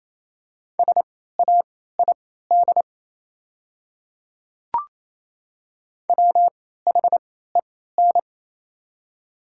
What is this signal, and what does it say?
Morse code, heard as a keyed tone, about 28 wpm, spaced as for 16 wpm, 690 Hz. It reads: SAID WHEN
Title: 3x Character Spacing - Top 100 Words - 28wpm